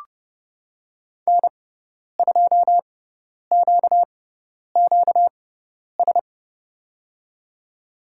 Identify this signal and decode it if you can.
Morse code, heard as a keyed tone, about 31 wpm, spaced as for 10 wpm, 690 Hz. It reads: N2QQS
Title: USA Call Signs - 5 Characters - 6x Character Spacing - 30wpm